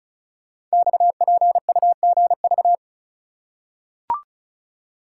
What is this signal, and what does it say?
Morse code, heard as a keyed tone, about 36 wpm, 700 Hz. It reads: XPUGV
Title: Code Groups - 5 Letters - 35wpm